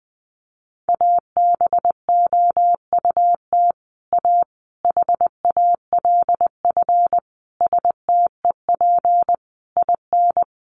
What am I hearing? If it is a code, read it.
Morse code, heard as a keyed tone, about 20 wpm, 700 Hz. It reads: ABOUT A HALF STEP IN
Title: Sentences from Top 1400 Words 20wpm